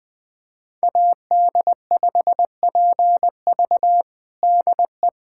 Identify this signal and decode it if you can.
Morse code, heard as a keyed tone, about 21 wpm, 700 Hz. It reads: AD5PV DE
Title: USA Callsign DE Callsign - 5 Characters - 20wpm